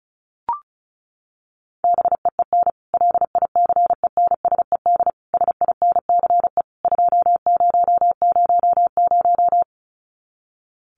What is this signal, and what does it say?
Morse code, heard as a keyed tone, about 34 wpm, 700 Hz. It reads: BEEN LICENSED SINCE 2000
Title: Common QSO Phrases 35wpm